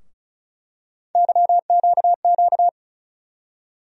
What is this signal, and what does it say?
Morse code, heard as a keyed tone, about 36 wpm, 700 Hz. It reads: YQQ
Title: USA Call Sign Postfixes - 3 Letters 35wpm